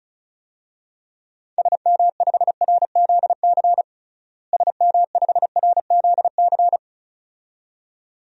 Morse code, heard as a keyed tone, about 36 wpm, 700 Hz. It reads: SM5RZC SM5RZC
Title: DXCC-100 Call Signs - 6 Characters - Repeated 2x - 35wpm